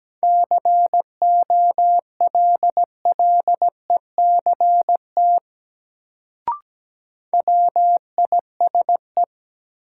Morse code, heard as a keyed tone, about 17 wpm, 700 Hz. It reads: COLLECT WISE
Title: Sentences from Top 1600 Words - New Word Drill - 17wpm